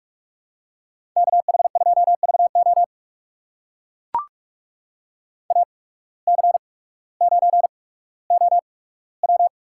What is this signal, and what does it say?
Morse code, heard as a keyed tone, about 44 wpm, spaced as for 19 wpm, 700 Hz. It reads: KH2VO AC9OW
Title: USA Call Signs - 5 Characters - 8x Character Spacing - 45wpm